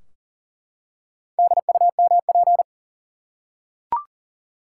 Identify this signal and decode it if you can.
Morse code, heard as a keyed tone, about 40 wpm, 700 Hz. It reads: DUMP
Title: Four Letter Words - 40wpm